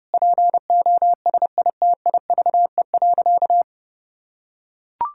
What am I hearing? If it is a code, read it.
Morse code, heard as a keyed tone, about 32 wpm, 700 Hz. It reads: POSITIVE.
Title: Sentences from Top 1000 Words - Review Part 5 - 30wpm